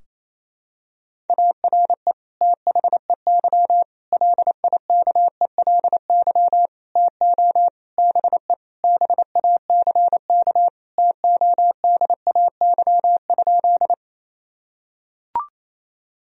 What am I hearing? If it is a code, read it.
Morse code, heard as a keyed tone, about 28 wpm, 700 Hz. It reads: ARE THEY LIKELY TO BE BACK TODAY?
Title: Sentences from Top 1000 Words - Review Part 4 - 28wpm